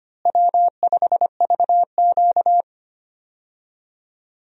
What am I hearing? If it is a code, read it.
Morse code, heard as a keyed tone, about 25 wpm, 700 Hz. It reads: W5VQ
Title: USA Call Signs - 4 Characters - Repeated 3x - 25wpm